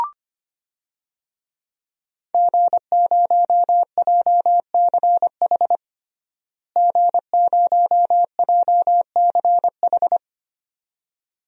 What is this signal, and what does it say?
Morse code, heard as a keyed tone, about 25 wpm, 700 Hz. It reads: G0JCH G0JCH
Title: DXCC-100 Call Signs - 5 Characters - Repeated 2x - 25wpm